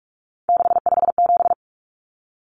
Morse code, heard as a keyed tone, about 44 wpm, 700 Hz. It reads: 657
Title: Common Cut Numbers - 3-Digit 45wpm